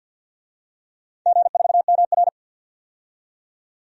Morse code, heard as a keyed tone, about 52 wpm, 700 Hz. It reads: G4MR
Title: DXCC-100 Call Signs - 4 Characters - 50wpm